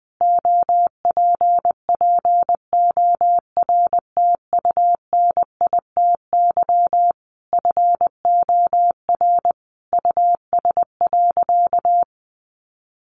Morse code, heard as a keyed tone, about 20 wpm, 700 Hz. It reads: OPPORTUNITY FOR US.
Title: Sentences from Top 1000 Words 20wpm